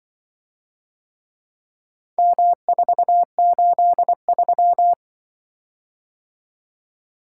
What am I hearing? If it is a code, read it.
Morse code, heard as a keyed tone, about 25 wpm, 700 Hz. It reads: M483
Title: Speed-Racing - Four Letter-Number Combinations - 20wpm